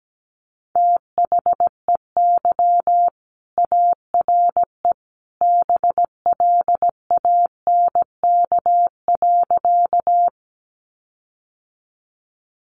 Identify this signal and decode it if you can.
Morse code, heard as a keyed tone, about 17 wpm, 700 Hz. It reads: THEY ARE BLANK.
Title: Sentences from Top 1500 Words 17wpm